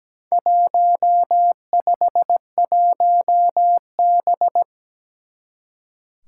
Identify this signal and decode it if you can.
Morse code, heard as a keyed tone, about 17 wpm, 700 Hz. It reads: 151B
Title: Four Challenging Letter-Number Combinations 17wpm